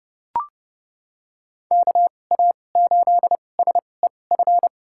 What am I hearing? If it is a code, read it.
Morse code, heard as a keyed tone, about 30 wpm, spaced as for 22 wpm, 700 Hz. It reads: KA8SEF
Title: USA Call Signs - 6 Characters - 2x Character Spacing - 30wpm